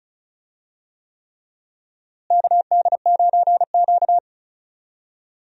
Morse code, heard as a keyed tone, about 34 wpm, 700 Hz. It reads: KD9Q
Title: USA Call Signs - 4 Characters - Repeated 3x - 35wpm